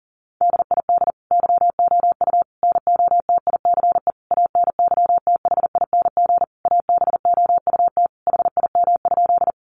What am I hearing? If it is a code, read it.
Morse code, heard as a keyed tone, about 40 wpm, 700 Hz. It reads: DID YOU NOTICE ANYTHING ABOUT HIM?
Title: Sentences from Top 1000 Words 40wpm